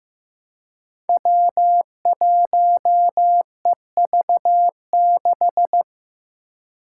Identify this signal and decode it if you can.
Morse code, heard as a keyed tone, about 16 wpm, 700 Hz. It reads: W1EV6
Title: Five Challenging Letter-Number Combinations 15wpm